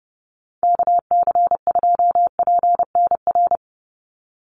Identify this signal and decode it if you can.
Morse code, heard as a keyed tone, about 30 wpm, 700 Hz. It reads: KC2PNR
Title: USA Call Signs 30wpm